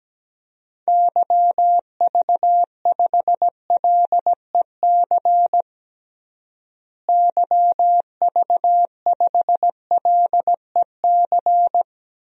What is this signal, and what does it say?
Morse code, heard as a keyed tone, about 17 wpm, 700 Hz. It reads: YV5LEC YV5LEC
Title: DXCC-100 Call Signs - 6 Characters - Repeated 3x - 17wpm